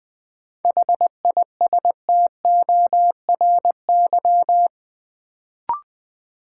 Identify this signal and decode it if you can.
Morse code, heard as a keyed tone, about 20 wpm, 700 Hz. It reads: HISTORY
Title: Sentences from Top 600 Words - New Word Drill - 20wpm